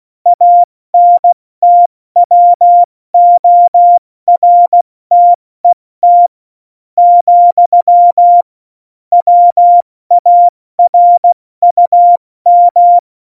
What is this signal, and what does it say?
Morse code, heard as a keyed tone, about 16 wpm, spaced as for 14 wpm, 700 Hz. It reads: ANTWORTET, WARUM
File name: Morsekurs / Witze / Witz 04. Juni 2019, 70 F80